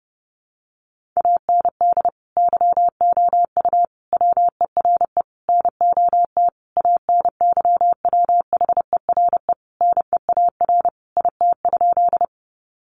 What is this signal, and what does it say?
Morse code, heard as a keyed tone, about 30 wpm, 700 Hz. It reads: AND YOU WERE NOT ANYWHERE NEAR IT?
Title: Sentences from Top 1400 Words 30wpm